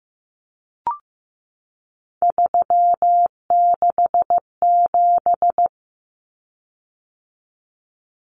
Morse code, heard as a keyed tone, about 15 wpm, 700 Hz. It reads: 367
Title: Common Cut Numbers - 3-Digit 15wpm